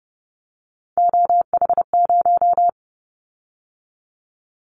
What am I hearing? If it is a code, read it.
Morse code, heard as a keyed tone, about 30 wpm, 700 Hz. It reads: OH0
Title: Callsign Prefixes for Top 100 DXCC - 30wpm